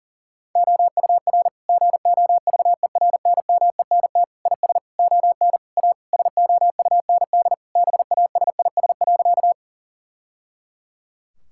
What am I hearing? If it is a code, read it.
Morse code, heard as a keyed tone, about 40 wpm, 700 Hz. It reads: OUR GOVERNMENT IS ON A SOUND BASIS.
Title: Sentences from Top 1000 Words - Review Part 5 - 40wpm